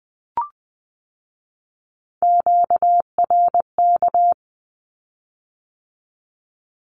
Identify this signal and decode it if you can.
Morse code, heard as a keyed tone, about 20 wpm, 700 Hz. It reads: QRK
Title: Q-Codes with Spoken Meaning - 20wpm